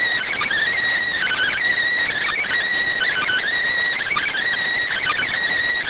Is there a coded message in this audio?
Yes: Emergency Alert System (SAME) tones from 1.1 to 5.3 s, heard as harsh buzzing bursts.